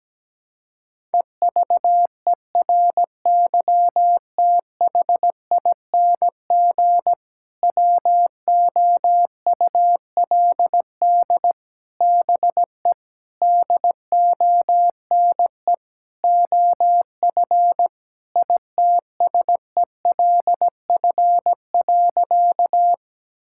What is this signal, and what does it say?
Morse code, heard as a keyed tone, about 17 wpm, 700 Hz. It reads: EVERYTHING WOULD BE DONE OF ITSELF.
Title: Sentences from Top 1000 Words - Review Part 5 - 17wpm